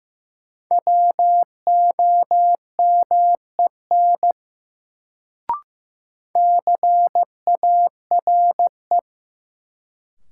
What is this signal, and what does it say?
Morse code, heard as a keyed tone, about 15 wpm, 700 Hz. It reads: WOMEN CARE
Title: Sets of 3 Words - New Word Drill - 15wpm